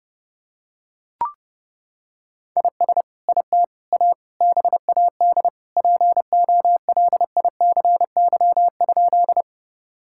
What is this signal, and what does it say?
Morse code, heard as a keyed tone, about 32 wpm, 700 Hz. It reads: IS IT A BAD POLICY?
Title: Sentences from Top 1000 Words - Review Part 5 - 30wpm